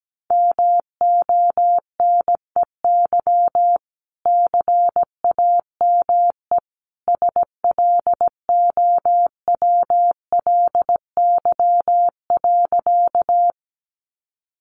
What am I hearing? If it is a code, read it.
Morse code, heard as a keyed tone, about 17 wpm, 700 Hz. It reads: MONEY CAME SLOWLY.
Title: Sentences from Top 1100 Words 17wpm